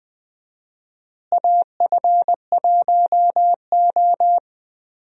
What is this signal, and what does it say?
Morse code, heard as a keyed tone, about 21 wpm, 700 Hz. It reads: AF1O
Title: USA Call Signs - 4 Characters - Repeated 2x - 20wpm